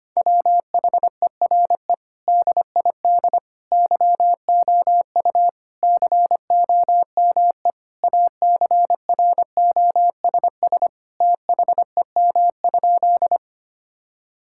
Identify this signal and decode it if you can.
Morse code, heard as a keyed tone, about 25 wpm, 700 Hz. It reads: WHERE DID YOU COME ACROSS THEM?
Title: Sentences from Top 1000 Words - Review Part 5 - 25wpm